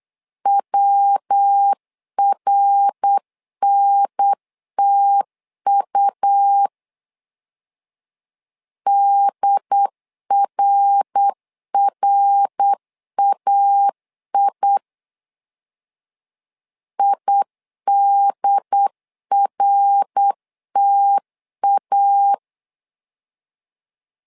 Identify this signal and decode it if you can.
Morse code, heard as a keyed tone, about 8 wpm, 800 Hz. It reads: WRNTU DRRAI IDRTA
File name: LEZIONI / 3 MWRO / 3 MWR 8wpm 2444